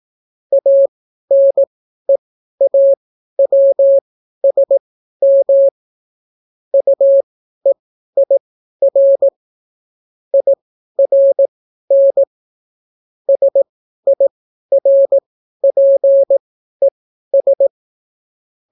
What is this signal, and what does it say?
Morse code, heard as a keyed tone, about 18 wpm, spaced as for 12 wpm, 550 Hz. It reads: ANEAWSM UEIR IRN SIRPES